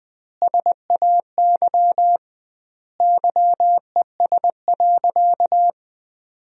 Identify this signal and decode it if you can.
Morse code, heard as a keyed tone, about 20 wpm, 700 Hz. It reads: SAY YES.
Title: 2x Word-Spacing - Sentences from Top 500 Words - 20wpm